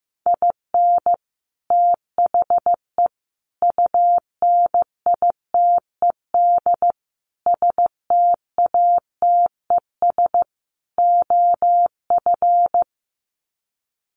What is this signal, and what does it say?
Morse code, heard as a keyed tone, about 15 wpm, 700 Hz. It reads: IN THE UNITED STATES OF
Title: Sets of 5 Words Encore2 15wpm